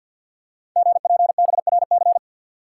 Morse code, heard as a keyed tone, about 46 wpm, 700 Hz. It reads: GPBRC
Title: Code Groups - 5 Letters - 50wpm